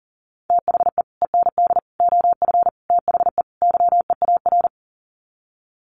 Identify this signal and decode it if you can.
Morse code, heard as a keyed tone, about 40 wpm, 700 Hz. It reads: THE END OF THE YEAR
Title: Sets of 5 Words 40wpm